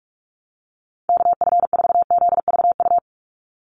Morse code, heard as a keyed tone, about 46 wpm, 700 Hz. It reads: KF4ZVU
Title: USA Call Signs 45wpm